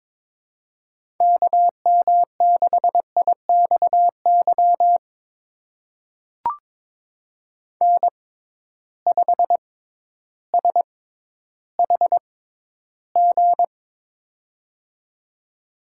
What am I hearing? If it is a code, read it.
Morse code, heard as a keyed tone, about 22 wpm, spaced as for 12 wpm, 700 Hz. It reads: KM6IXY N5SHG
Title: USA Call Signs - 6x Character Spacing - 22wpm